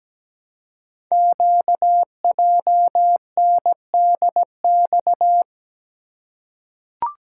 Morse code, heard as a keyed tone, about 17 wpm, 700 Hz. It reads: QJNDX
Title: Code Groups - 5 Letters - 17wpm